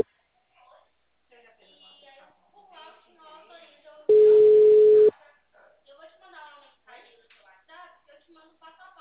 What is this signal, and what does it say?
Ringback tone (Europe, 425 Hz), with speech.